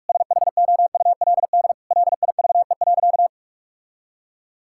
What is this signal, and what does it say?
Morse code, heard as a keyed tone, about 43 wpm, 700 Hz. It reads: SHOULD LIVE.
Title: Sentences from Top 1000 Words - Review Part 5 - 45wpm